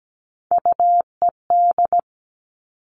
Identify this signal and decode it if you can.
Morse code, heard as a keyed tone, about 17 wpm, 700 Hz. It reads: UED
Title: Three Challenging Letter-Number Combinations 17wpm